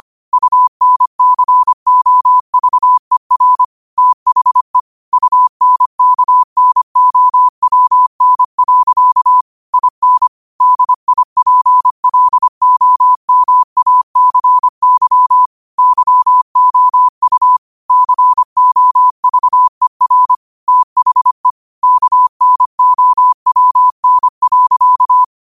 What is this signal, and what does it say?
Morse code, heard as a keyed tone, about 25 wpm, 1000 Hz. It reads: UNCOVER THE UNKNOWN. IN DIPLOMACY YOU COVER THE KNOWN.
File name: qotdfiles / 25wpm